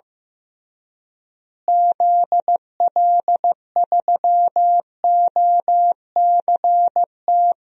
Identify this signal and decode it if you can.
Morse code, heard as a keyed tone, about 15 wpm, 700 Hz. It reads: ZL3OCT
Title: DXCC-100 Call Signs - 6 Characters - Repeated 3x - 15wpm